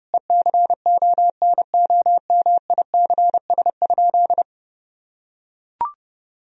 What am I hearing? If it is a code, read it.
Morse code, heard as a keyed tone, about 30 wpm, 700 Hz. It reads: ECONOMICS?
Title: Sentences from Top 1000 Words - Review Part 3 - 30wpm